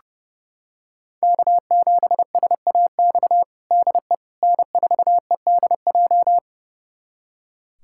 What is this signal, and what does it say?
Morse code, heard as a keyed tone, about 30 wpm, 700 Hz. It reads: K7SAX DE N4EDJ
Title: USA Callsign DE Callsign - 5 Characters - 30wpm